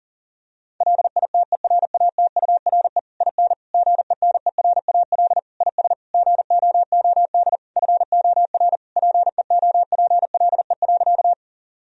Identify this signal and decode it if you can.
Morse code, heard as a keyed tone, about 40 wpm, 700 Hz. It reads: LITERATURE IN GENERAL IS GOOD FOR PEOPLE.
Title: Sentences from Top 1000 Words - Review Part 3 - 40wpm